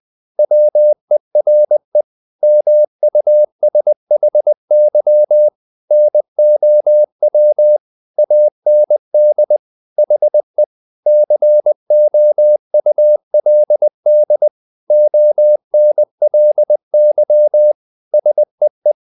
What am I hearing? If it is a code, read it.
Morse code, heard as a keyed tone, about 20 wpm, 600 Hz. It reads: WERE MUSHY NOW AND HE COULD ONLY SEE
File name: the old man and the sea part 2 20wpm